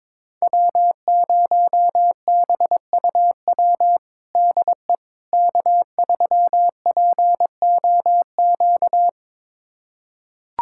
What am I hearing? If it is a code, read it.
Morse code, heard as a keyed tone, about 22 wpm, 700 Hz. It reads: W0BUW DE K3POQ
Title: USA Callsign DE Callsign - 5 Characters - 22wpm